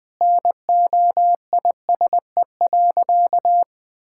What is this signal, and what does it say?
Morse code, heard as a keyed tone, about 20 wpm, 700 Hz. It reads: NOISE.
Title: Sentences with 3-Word Phrases - Spoken Beginning - 20wpm